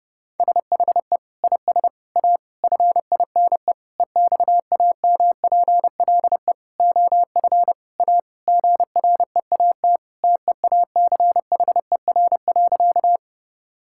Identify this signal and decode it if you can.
Morse code, heard as a keyed tone, about 30 wpm, 700 Hz. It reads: SHE IS A FINE EXAMPLE OF A GREAT TEACHER.
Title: Sentences from Top 1000 Words - Review Part 6 - 30wpm